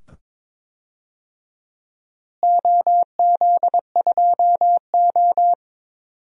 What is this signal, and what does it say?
Morse code, heard as a keyed tone, about 22 wpm, 700 Hz. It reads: OZ2O
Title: DXCC-100 Call Signs - 4 Characters - Repeated 2x - 22wpm